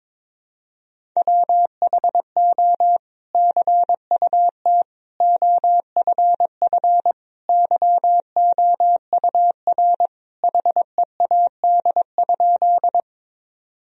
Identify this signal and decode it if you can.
Morse code, heard as a keyed tone, about 22 wpm, 700 Hz. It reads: WHO CUT OFF YOUR HEAD?
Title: Sentences from Top 1000 Words - Review Part 1 - 22wpm